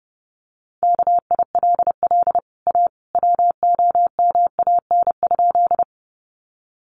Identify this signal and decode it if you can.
Morse code, heard as a keyed tone, about 29 wpm, 700 Hz. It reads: KILL A WOMAN?
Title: Sentences from Top 1400 Words 30wpm